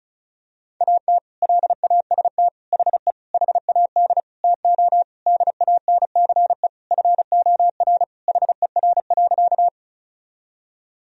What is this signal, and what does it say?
Morse code, heard as a keyed tone, about 36 wpm, 700 Hz. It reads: AT LAST HE HAD TO DANCE FOR HER.
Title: Sentences from Top 1000 Words - Review Part 4 - 35wpm